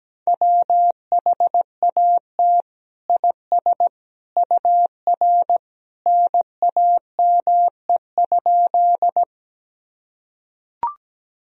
Morse code, heard as a keyed tone, about 17 wpm, 700 Hz. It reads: WHAT IS UR NAME?
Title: Sentences from Top 200 Words and Abbreviations 17wpm